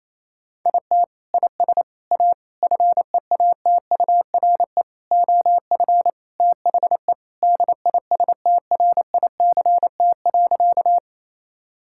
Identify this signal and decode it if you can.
Morse code, heard as a keyed tone, about 27 wpm, 700 Hz. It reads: IT IS A FEATURE OF THE DISTRICT.
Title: Sentences from Top 1600 Words 28wpm